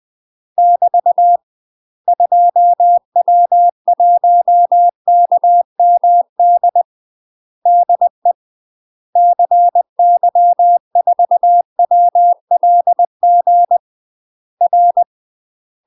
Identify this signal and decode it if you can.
Morse code, heard as a keyed tone, about 20 wpm, 700 Hz. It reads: <BT> 2W1KMD DE CY4WLG R
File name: UK MASTER QSOs TRAINING/20WPM